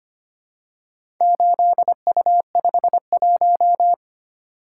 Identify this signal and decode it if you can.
Morse code, heard as a keyed tone, about 25 wpm, 700 Hz. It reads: 8U51
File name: practice / 25wpm / Four Letter-Number Combinations - 25wpm